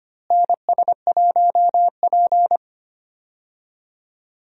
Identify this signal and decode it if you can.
Morse code, heard as a keyed tone, about 25 wpm, 700 Hz. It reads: NS1P
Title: USA Call Signs - 4 Characters - Repeated 3x - 25wpm